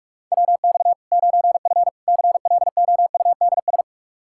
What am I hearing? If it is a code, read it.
Morse code, heard as a keyed tone, about 41 wpm, 700 Hz. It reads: WX 9F CLOUDS
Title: K1USN SST QSO Practice 45wpm